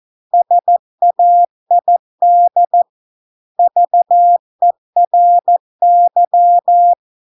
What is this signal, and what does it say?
Morse code, heard as a keyed tone, about 14 wpm, 700 Hz. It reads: SAID VERY